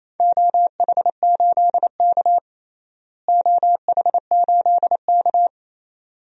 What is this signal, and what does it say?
Morse code, heard as a keyed tone, about 28 wpm, 700 Hz. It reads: OH8K OH8K
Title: DXCC-100 Call Signs - 4 Characters - Repeated 2x - 28wpm